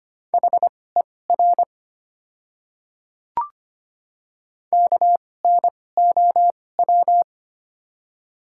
Morse code, heard as a keyed tone, about 25 wpm, spaced as for 18 wpm, 700 Hz. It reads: HER KNOW
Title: 2x Character Spacing - Top 100 Words - 25wpm